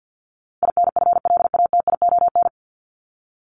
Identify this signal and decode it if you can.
Morse code, heard as a keyed tone, about 49 wpm, 700 Hz. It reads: INFLATION
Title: Top 1001-1250 Words 50wpm